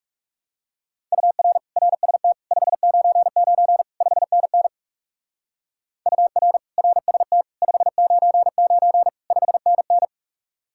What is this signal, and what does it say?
Morse code, heard as a keyed tone, about 36 wpm, 700 Hz. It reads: UR RST 599 5NN UR RST 599 5NN
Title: Speed-Racing Common QSO Phrases Encore - 30wpm